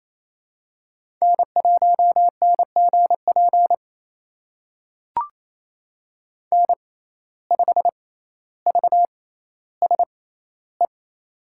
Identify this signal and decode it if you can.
Morse code, heard as a keyed tone, about 28 wpm, spaced as for 14 wpm, 700 Hz. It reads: N1NGP N5VSE
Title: USA Call Signs - 5 Characters - 6x Character Spacing - 28wpm